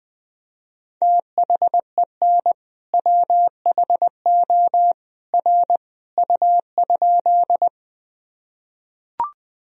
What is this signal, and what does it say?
Morse code, heard as a keyed tone, about 20 wpm, 700 Hz. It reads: THEN WHO R U?
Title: Sentences from Top 100 Words and Abbreviations 20wpm